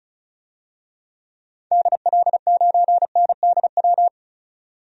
Morse code, heard as a keyed tone, about 36 wpm, 700 Hz. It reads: DL9NDW
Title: DXCC-100 Call Signs - 6 Characters - 35wpm